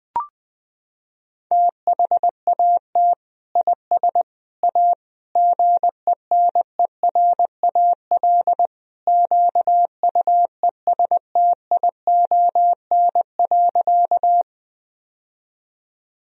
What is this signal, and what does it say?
Morse code, heard as a keyed tone, about 21 wpm, 700 Hz. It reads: THAT IS A GENERAL QUESTION.
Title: Sentences from Top 1000 Words - Review Part 3 - 20wpm